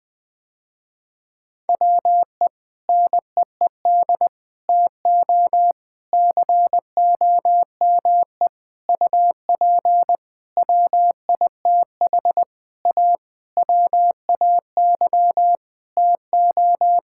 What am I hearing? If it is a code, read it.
Morse code, heard as a keyed tone, about 19 wpm, 700 Hz. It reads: WE NEED TO COME UP WITH A WAY TO
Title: Sentences with 7-Word Phrases - Spoken Beginning - 20wpm